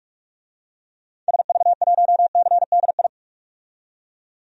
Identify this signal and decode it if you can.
Morse code, heard as a keyed tone, about 42 wpm, 700 Hz. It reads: SV1CDI
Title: DXCC-100 Call Signs - 6 Characters - 45wpm